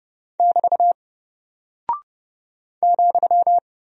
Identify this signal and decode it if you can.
Morse code, heard as a keyed tone, about 30 wpm, 700 Hz. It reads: <BT> ,
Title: Single Character (Letter:Number:Punctuation:Prosign) Alt 30wpm